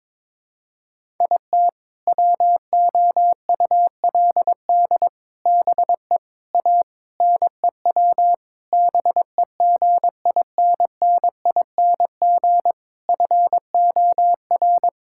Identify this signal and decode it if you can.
Morse code, heard as a keyed tone, about 22 wpm, 700 Hz. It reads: IT WOULD BE A NEW BEGINNING FOR
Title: Sentences from Top 1000 Words - Review Part 5 - 22wpm